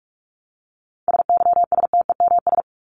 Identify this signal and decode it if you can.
Morse code, heard as a keyed tone, about 44 wpm, 700 Hz. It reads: SYSTEMS
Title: ES:S Suffix - Top 100 Words with Base Words 45wpm